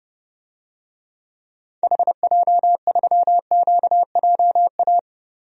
Morse code, heard as a keyed tone, about 30 wpm, 700 Hz. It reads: HJ3QJA
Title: DXCC-100 Call Signs - 6 Characters - Repeated 2x - 30wpm